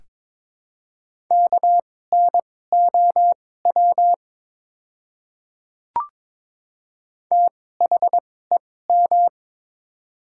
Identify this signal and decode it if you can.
Morse code, heard as a keyed tone, about 21 wpm, spaced as for 16 wpm, 700 Hz. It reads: KNOW THEM
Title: 2x Character Spacing - Top 100 Words - 22wpm